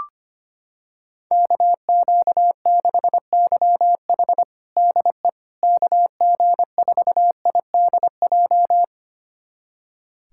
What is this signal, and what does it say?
Morse code, heard as a keyed tone, about 25 wpm, 700 Hz. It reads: KQ6YH DE KG4IDJ
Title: USA Callsign DE Callsign 25wpm